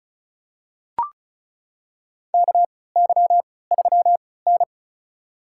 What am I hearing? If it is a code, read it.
Morse code, heard as a keyed tone, about 36 wpm, spaced as for 20 wpm, 700 Hz. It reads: KY3N
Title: USA Call Signs - 4 Characters - 3x Character Spacing - 35wpm